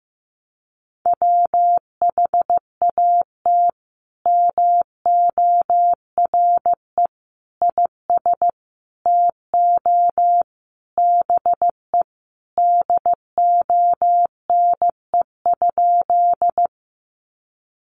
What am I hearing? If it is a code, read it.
Morse code, heard as a keyed tone, about 15 wpm, 700 Hz. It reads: WHAT MORE IS TO BE DONE?
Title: Sentences from Top 500 Words 15wpm